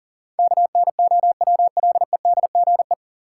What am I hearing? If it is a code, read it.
Morse code, heard as a keyed tone, about 40 wpm, 700 Hz. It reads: KNOWLEDGE
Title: Sentences from Top 600 Words - New Word Drill - 40wpm